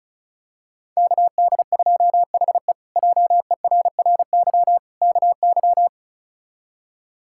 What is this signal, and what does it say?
Morse code, heard as a keyed tone, about 36 wpm, 700 Hz. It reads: KD2HE JERRY KY
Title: K1USN SST QSO Practice - 35wpm